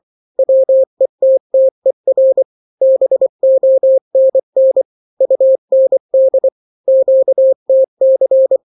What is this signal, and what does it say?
Morse code, heard as a keyed tone, about 24 wpm, 530 Hz. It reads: WETTER BONN UND QTC
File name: daily / 23wpm